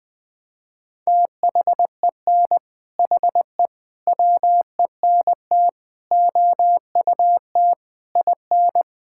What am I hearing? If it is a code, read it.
Morse code, heard as a keyed tone, about 20 wpm, 700 Hz. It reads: THEN HE WENT OUT IN
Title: Sentences from Top 1000 Words - Review Part 5 - 20wpm